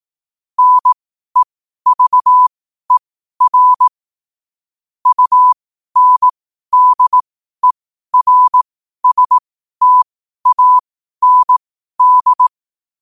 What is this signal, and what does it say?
Morse code, heard as a keyed tone, about 18 wpm, spaced as for 12 wpm, 1000 Hz. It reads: NEVER UNDERSTAND
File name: qotdfiles / 12wpm